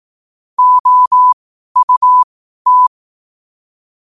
Morse code, heard as a keyed tone, about 18 wpm, spaced as for 12 wpm, 1000 Hz. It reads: OUT
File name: 12wpm